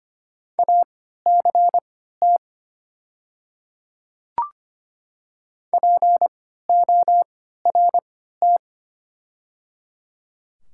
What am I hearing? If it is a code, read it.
Morse code, heard as a keyed tone, about 25 wpm, spaced as for 14 wpm, 690 Hz. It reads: ACT PORT
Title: 3x Character Spacing - Top 200 Words - 25wpm